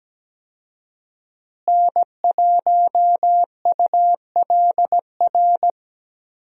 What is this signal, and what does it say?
Morse code, heard as a keyed tone, about 17 wpm, 700 Hz. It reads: N1ULR
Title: USA Call Signs - 5 Characters - Repeated 2x - 17wpm